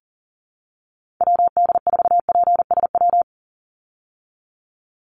Morse code, heard as a keyed tone, about 40 wpm, 700 Hz. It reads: WD4PSW
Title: USA Call Signs 40wpm